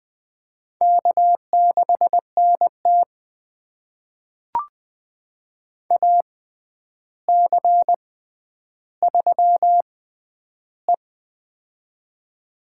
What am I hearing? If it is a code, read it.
Morse code, heard as a keyed tone, about 20 wpm, spaced as for 10 wpm, 700 Hz. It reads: K6NT AC3E
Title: USA Call Signs - 4 Characters - 6x Character Spacing - 20wpm